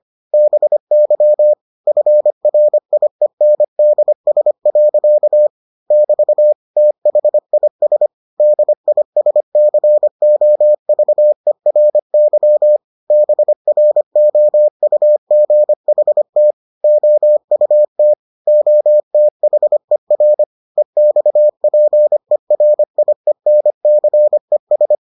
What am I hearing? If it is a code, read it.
Morse code, heard as a keyed tone, about 25 wpm, 600 Hz. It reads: BY FRIENDS. <BT> THIS DISCOVERY BROUGHT OUT OTHER EXPERIENCES